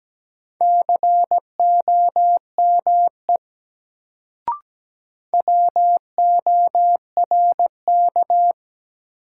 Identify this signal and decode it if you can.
Morse code, heard as a keyed tone, about 17 wpm, 700 Hz. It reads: COME WORK
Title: Sets of 2 Words - New Word Drill - 17wpm